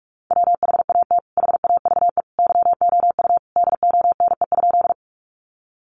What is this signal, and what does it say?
Morse code, heard as a keyed tone, about 44 wpm, 700 Hz. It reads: WHAT HAVE YOU DONE?
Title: Sentences from Top 500 Words 45wpm